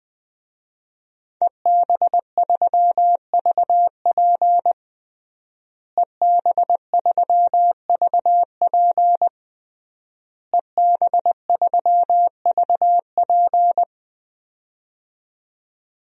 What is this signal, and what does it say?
Morse code, heard as a keyed tone, about 20 wpm, 700 Hz. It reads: EB3VP EB3VP EB3VP
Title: DXCC-100 Call Signs - 5 Characters - Repeated 3x - 20wpm